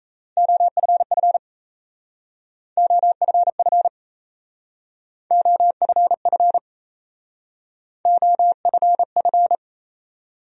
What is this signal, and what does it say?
Morse code, heard as a keyed tone, about 37 wpm, 700 Hz. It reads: OFF OFF OFF OFF